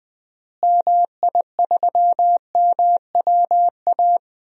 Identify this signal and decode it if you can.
Morse code, heard as a keyed tone, about 20 wpm, 700 Hz. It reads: MI3MWA
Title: DXCC-100 Call Signs - 6 Characters - Repeated 3x - 20wpm